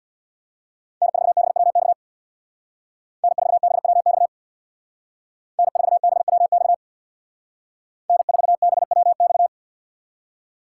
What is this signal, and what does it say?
Morse code, heard as a keyed tone, about 62 wpm, 700 Hz. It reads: N4BWX N4BWX N4BWX N4BWX
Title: Speed Racing - USA Call Signs 50wpm